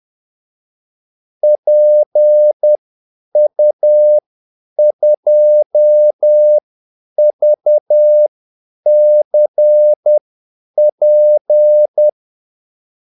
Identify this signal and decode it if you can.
Morse code, heard as a keyed tone, about 10 wpm, spaced as for 8 wpm, 600 Hz. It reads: PU2VCP
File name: Callsigns-10-8wpm